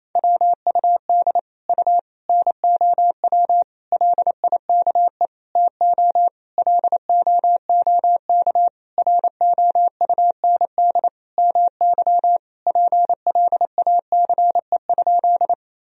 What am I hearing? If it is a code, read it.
Morse code, heard as a keyed tone, about 28 wpm, 700 Hz. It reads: WUD U NOW LIKE TO LOOK ROUND MY PLACE?
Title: Sentences from Top 200 Words and Abbreviations 28wpm